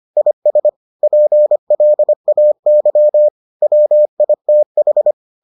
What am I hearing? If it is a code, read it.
Morse code, heard as a keyed tone, about 25 wpm, 600 Hz. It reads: IS PLAY WITH